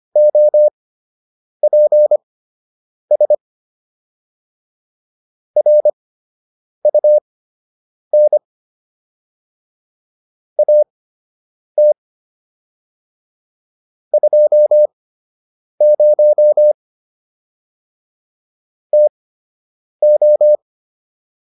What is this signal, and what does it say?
Morse code, heard as a keyed tone, about 25 wpm, spaced as for 8 wpm, 600 Hz. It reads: OPS RUN AT 20 TO